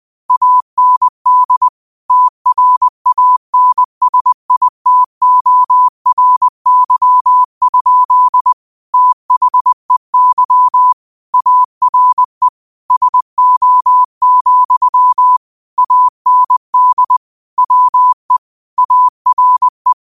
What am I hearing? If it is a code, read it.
Morse code, heard as a keyed tone, about 20 wpm, 1000 Hz. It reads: AND TRANSITORY? THEY ARE SO, AND WE ARE
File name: qotdfiles / 20wpm